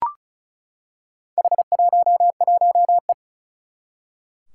Morse code, heard as a keyed tone, about 36 wpm, 700 Hz. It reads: H11E
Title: Four Challenging Letter-Number Combinations 35wpm